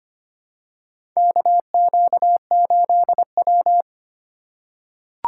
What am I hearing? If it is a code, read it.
Morse code, heard as a keyed tone, about 25 wpm, 700 Hz. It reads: KQ8W
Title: USA Call Signs - 4 Characters - 4x Character Spacing - 25wpm